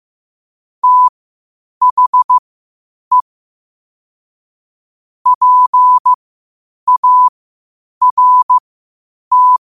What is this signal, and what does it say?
Morse code, heard as a keyed tone, about 15 wpm, spaced as for 8 wpm, 1000 Hz. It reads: THE PART
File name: qotdfiles / 8wpm